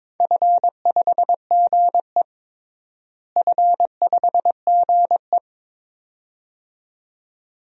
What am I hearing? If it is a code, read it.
Morse code, heard as a keyed tone, about 22 wpm, 700 Hz. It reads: F5GE F5GE
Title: DXCC-100 Call Signs - 4 Characters - Repeated 2x - 22wpm